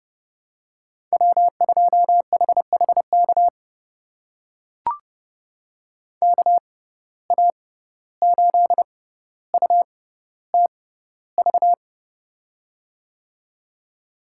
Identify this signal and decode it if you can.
Morse code, heard as a keyed tone, about 30 wpm, spaced as for 15 wpm, 700 Hz. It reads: W2HHK KA8UTV
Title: USA Call Signs - 6x Character Spacing - 30wpm